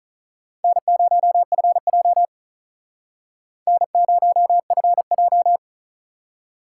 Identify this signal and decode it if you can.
Morse code, heard as a keyed tone, about 47 wpm, 700 Hz. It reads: N0FJ N0FJ